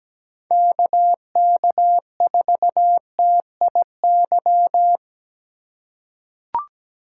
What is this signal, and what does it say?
Morse code, heard as a keyed tone, about 17 wpm, 700 Hz. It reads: KK4TIY
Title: USA Call Signs - 4x Character Spacing - 17wpm